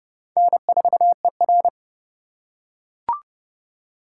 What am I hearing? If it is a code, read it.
Morse code, heard as a keyed tone, about 30 wpm, 700 Hz. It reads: N4ER
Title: USA Call Signs - 4 Characters - 2x Character Spacing - 30wpm